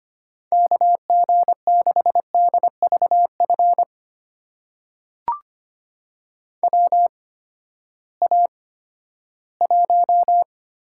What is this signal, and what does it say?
Morse code, heard as a keyed tone, about 25 wpm, spaced as for 14 wpm, 700 Hz. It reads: KG6DVF WA1
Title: USA Call Signs - 6 Characters - 8x Character Spacing - 25wpm